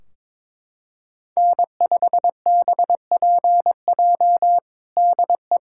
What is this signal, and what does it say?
Morse code, heard as a keyed tone, about 22 wpm, 700 Hz. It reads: N5BPJ DE
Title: K1USN SST QSO Practice 22wpm